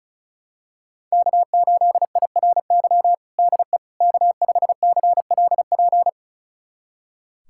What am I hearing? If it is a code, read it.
Morse code, heard as a keyed tone, about 34 wpm, 700 Hz. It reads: K8IRY DE K5CLP
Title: USA Callsign DE Callsign - 5 Characters - 35wpm